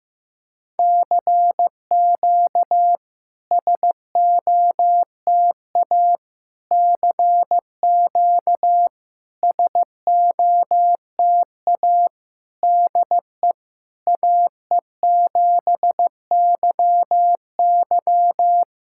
Morse code, heard as a keyed tone, about 15 wpm, 700 Hz. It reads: CQ SOTA CQ SOTA DE AE7YY
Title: SOTA QSO Practice 15wpm